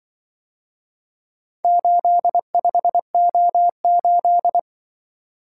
Morse code, heard as a keyed tone, about 24 wpm, 700 Hz. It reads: 85O8
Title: Speed-Racing - Four Letter-Number Combinations - 20wpm